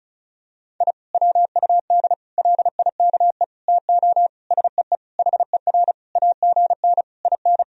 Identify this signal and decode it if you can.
Morse code, heard as a keyed tone, about 34 wpm, 700 Hz. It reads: I WUD LIKE TO SEE HER AGN IN
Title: Sentences from Top 200 Words and Abbreviations 35wpm